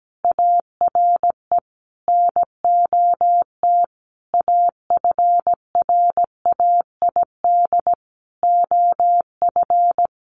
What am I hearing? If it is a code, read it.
Morse code, heard as a keyed tone, about 17 wpm, 700 Hz. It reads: ARE NOT AFRAID OF
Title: Sentences from Top 1500 Words 17wpm